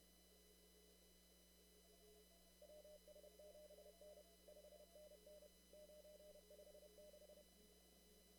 Morse code, heard as a keyed tone, about 30 wpm, 600 Hz. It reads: WS7N 5NN 956
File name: CW / kt5j wpx cw-op-k5pi